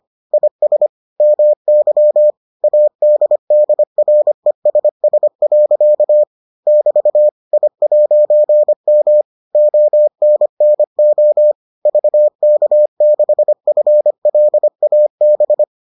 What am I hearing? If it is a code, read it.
Morse code, heard as a keyed tone, about 25 wpm, 600 Hz. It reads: IS MY ADDRESS. <BT> I'M ONNO VK6FLAB